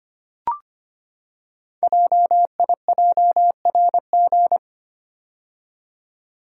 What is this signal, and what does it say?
Morse code, heard as a keyed tone, about 25 wpm, 700 Hz. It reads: JIJRG
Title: Code Groups - 5 Letters - 25wpm